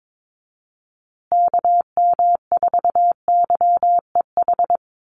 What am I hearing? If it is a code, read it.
Morse code, heard as a keyed tone, about 22 wpm, 700 Hz. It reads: KM4YEH
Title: USA Call Signs Repeated 2x 22wpm